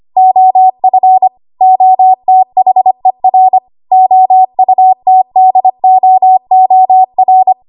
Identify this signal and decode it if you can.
Morse code, heard as a keyed tone, about 25 wpm, 750 Hz. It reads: OF OTHER OUTDOOR
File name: Morse / 25WPM